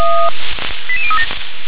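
Five-tone selective call (ZVEI), heard as quick runs of five tones, tones of about 70 ms. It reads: 88037
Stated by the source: Five-tone ZVEI call received on an SDR.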